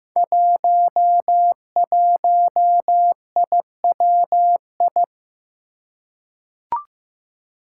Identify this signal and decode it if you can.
Morse code, heard as a keyed tone, about 15 wpm, 700 Hz. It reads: 11IWI
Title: Five Challenging Letter-Number Combinations 15wpm